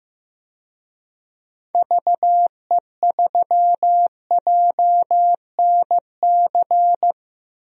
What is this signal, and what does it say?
Morse code, heard as a keyed tone, about 15 wpm, 700 Hz. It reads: VE3JNC
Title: DXCC-100 Call Signs - 6 Characters - Repeated 2x - 15wpm